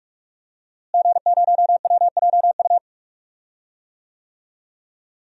Morse code, heard as a keyed tone, about 46 wpm, 700 Hz. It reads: G0WJU
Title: DXCC-100 Call Signs - 5 Characters - Repeated 3x - 45wpm